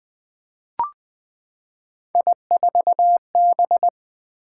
Morse code, heard as a keyed tone, about 20 wpm, 700 Hz. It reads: I4B
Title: Three Letter-Number Combinations - 20wpm